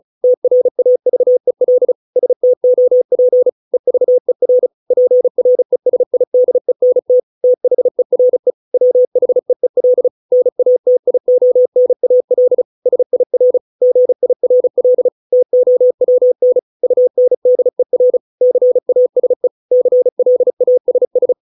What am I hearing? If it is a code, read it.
Morse code, heard as a keyed tone, about 34 wpm, 500 Hz. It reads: TRAVEL STOP EVER PRESIDENT THERE WHEEL NATIONAL SIR GIRL TOWN UNDER CASE CLASS